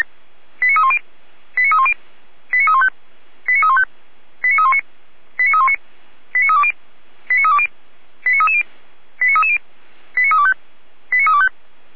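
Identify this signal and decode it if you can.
Five-tone selective call (ZVEI), heard as quick runs of five tones, tones of about 70 ms. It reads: 89310 89310 89315 89315 89319 89319 89320 89320 89330 89330 89325 89325
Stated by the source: ZVEI five-tone selective call received on SDR, fire brigade Orlamünde.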